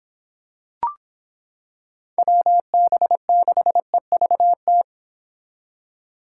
Morse code, heard as a keyed tone, about 25 wpm, 700 Hz. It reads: WB6EVT